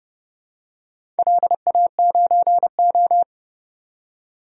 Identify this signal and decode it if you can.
Morse code, heard as a keyed tone, about 31 wpm, 700 Hz. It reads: LA9O
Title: DXCC-100 Call Signs - 4 Characters - 30wpm